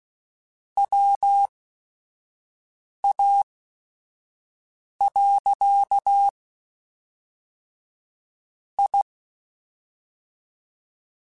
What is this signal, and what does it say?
Morse code, heard as a keyed tone, about 16 wpm, spaced as for 5 wpm, 800 Hz. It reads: WA. I